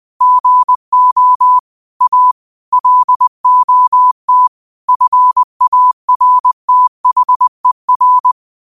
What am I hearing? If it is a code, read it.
Morse code, heard as a keyed tone, about 20 wpm, 1000 Hz. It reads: GO A LOT FARTHER